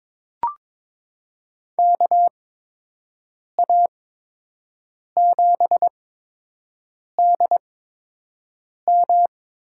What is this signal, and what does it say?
Morse code, heard as a keyed tone, about 22 wpm, spaced as for 6 wpm, 700 Hz. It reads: KA7DM
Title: USA Call Signs - 8x Character Spacing - 22wpm